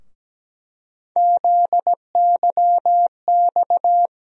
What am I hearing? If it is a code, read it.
Morse code, heard as a keyed tone, about 17 wpm, 700 Hz. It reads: ZYX
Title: USA Call Sign Postfixes - 3 Letters 17wpm